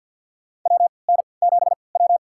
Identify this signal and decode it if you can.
Morse code, heard as a keyed tone, about 46 wpm, spaced as for 27 wpm, 700 Hz. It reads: WN7W
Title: USA Call Signs - 4 Characters - 3x Character Spacing - 50wpm